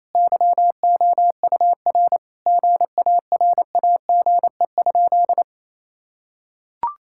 Morse code, heard as a keyed tone, about 28 wpm, 700 Hz. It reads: YOUR GARAGE?
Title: Sentences from Top 1600 Words 28wpm